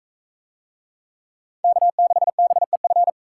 Morse code, heard as a keyed tone, about 39 wpm, 700 Hz. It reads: K6BEF